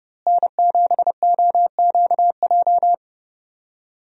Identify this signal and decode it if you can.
Morse code, heard as a keyed tone, about 32 wpm, 700 Hz. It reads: N7OQJ
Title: USA Call Signs - 5 Characters - Repeated 3x - 30wpm